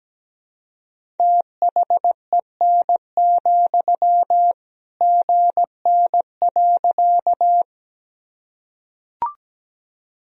Morse code, heard as a keyed tone, about 17 wpm, 700 Hz. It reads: THEN, GN.
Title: Sentences from Top 300 Words and Abbreviations 17wpm